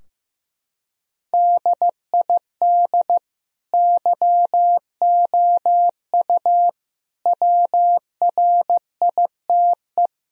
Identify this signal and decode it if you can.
Morse code, heard as a keyed tone, about 16 wpm, 700 Hz. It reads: DID YOU WRITE
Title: Sentences from Top 1000 Words - Review Part 3 - 15wpm